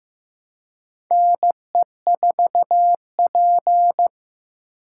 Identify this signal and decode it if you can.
Morse code, heard as a keyed tone, about 15 wpm, 700 Hz. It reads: NE4P
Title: USA Call Signs - 4x Character Spacing - 15wpm